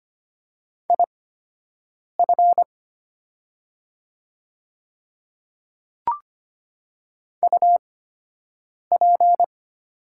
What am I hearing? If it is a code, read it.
Morse code, heard as a keyed tone, about 25 wpm, spaced as for 7 wpm, 700 Hz. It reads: IF UP
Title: 8x Character Spacing - Top 100 Words - 25wpm